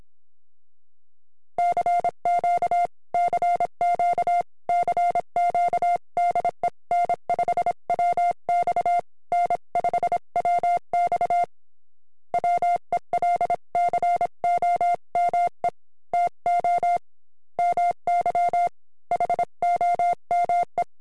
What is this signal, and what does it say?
Morse code, heard as a keyed tone, about 26 wpm, 700 Hz. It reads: CQ CQ CQ DE N5WX N5WX WELCOME TO MY HOME